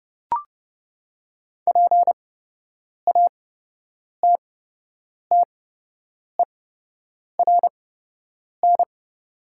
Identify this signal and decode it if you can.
Morse code, heard as a keyed tone, about 30 wpm, spaced as for 8 wpm, 700 Hz. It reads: PATTERN
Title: 8x Character Spacing - Top 500 Words - 30wpm